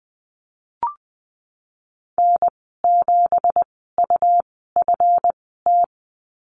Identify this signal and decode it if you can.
Morse code, heard as a keyed tone, about 20 wpm, spaced as for 15 wpm, 700 Hz. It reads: N7UFT
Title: 167 Lesson 27g - Number 7 - Call Signs 20wpm